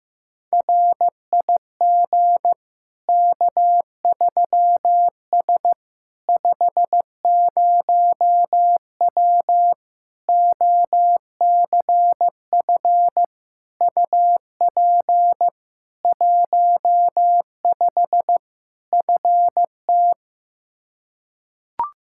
Morse code, heard as a keyed tone, about 15 wpm, 700 Hz. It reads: RIG K3S 50W OCF UP 15 FT
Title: Practice QSO - Exchange 2 - Info - 15wpm